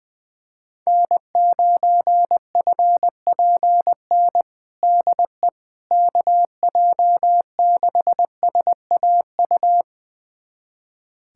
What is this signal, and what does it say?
Morse code, heard as a keyed tone, about 21 wpm, 700 Hz. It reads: N9FPN DE KJ6SAU
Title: USA Callsign DE Callsign 20wpm